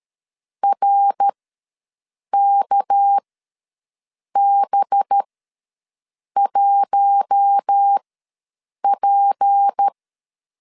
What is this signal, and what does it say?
Morse code, heard as a keyed tone, about 13 wpm, spaced as for 6 wpm, 800 Hz. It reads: RKB1P
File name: LEZIONI / 6 PKX0 / 6 PK ws 12wpm 3116